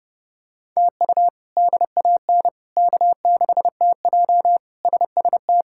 Morse code, heard as a keyed tone, about 30 wpm, 700 Hz. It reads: TU DAN K6TJ SST
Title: K1USN SST QSO Practice - 30wpm